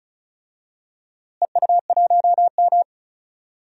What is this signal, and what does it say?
Morse code, heard as a keyed tone, about 36 wpm, 700 Hz. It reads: EU1M
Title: DXCC-100 Call Signs - 4 Characters - Repeated 2x - 35wpm